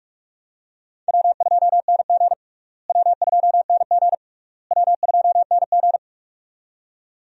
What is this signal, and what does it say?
Morse code, heard as a keyed tone, about 42 wpm, 700 Hz. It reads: W2NG W2NG W2NG
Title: USA Call Signs - 4 Characters - Repeated 3x - 45wpm